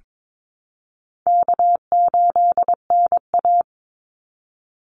Morse code, heard as a keyed tone, about 22 wpm, 700 Hz. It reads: K8NA
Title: Easy Call Signs Repeated 3x - 22wpm